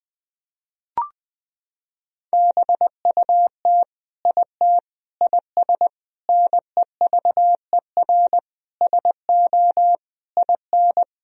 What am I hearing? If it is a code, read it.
Morse code, heard as a keyed tone, about 20 wpm, 700 Hz. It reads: BUT IT IS NEVER SO IN
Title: Sentences from Top 1000 Words - Review Part 2 - 20wpm